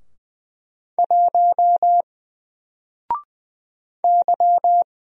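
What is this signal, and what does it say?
Morse code, heard as a keyed tone, about 20 wpm, 700 Hz. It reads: 1 Y
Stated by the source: Single letter and number Morse drill at 20 wpm, from N4LCD.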